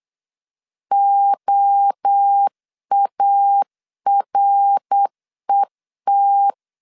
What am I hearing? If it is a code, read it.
Morse code, heard as a keyed tone, about 8 wpm, 800 Hz. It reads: OARET